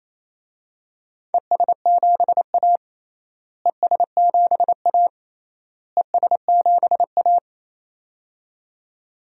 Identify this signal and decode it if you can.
Morse code, heard as a keyed tone, about 28 wpm, 700 Hz. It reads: ES7A ES7A ES7A
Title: DXCC-100 Call Signs - 4 Characters - Repeated 3x - 28wpm